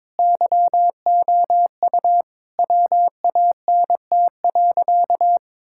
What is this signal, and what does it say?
Morse code, heard as a keyed tone, about 21 wpm, 700 Hz. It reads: YOU WANT.
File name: practice / 22wpm / Sentences with 3-Word Phrases - 22wpm